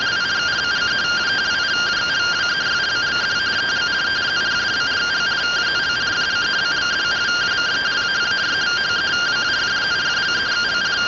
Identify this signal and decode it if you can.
Two-tone teleprinter signal, about 170 Hz shift.